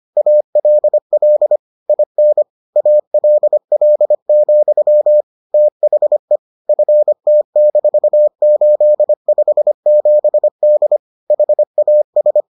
Morse code, heard as a keyed tone, about 25 wpm, 600 Hz. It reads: ALL IN ALL, THE FT-857D HAS